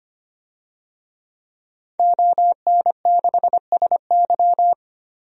Morse code, heard as a keyed tone, about 25 wpm, 700 Hz. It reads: ON6SY
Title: DXCC-100 Call Signs - 5 Characters - 25wpm